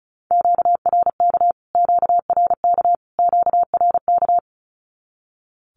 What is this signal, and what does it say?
Morse code, heard as a keyed tone, about 34 wpm, 700 Hz. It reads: QRK QRK QRK
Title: Q-Codes Repeated 3x 35wpm